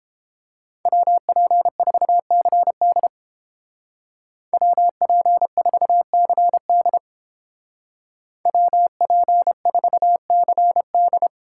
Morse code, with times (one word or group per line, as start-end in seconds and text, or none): 0.9-3.1 WP4CD
4.5-7.0 WP4CD
8.5-11.3 WP4CD